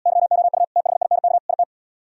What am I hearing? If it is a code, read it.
Morse code, heard as a keyed tone, about 77 wpm, 700 Hz. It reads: YOU THINK IT